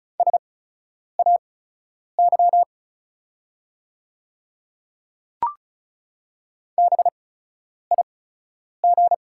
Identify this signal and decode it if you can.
Morse code, heard as a keyed tone, about 34 wpm, spaced as for 10 wpm, 690 Hz. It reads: SAY BIG
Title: 8x Character Spacing - Top 200 Words - 35wpm